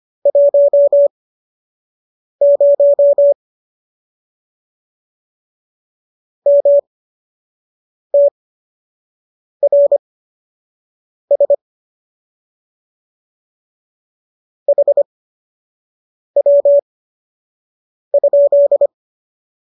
Morse code, heard as a keyed tone, about 25 wpm, spaced as for 6 wpm, 570 Hz. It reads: 10 MTRS HW?
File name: qso002 06